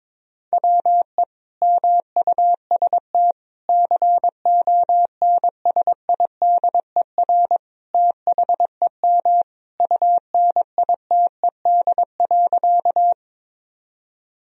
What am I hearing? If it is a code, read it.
Morse code, heard as a keyed tone, about 22 wpm, 700 Hz. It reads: WE MUST CONSIDER THEM UNITED.
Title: Sentences from Top 1000 Words - Review Part 4 - 22wpm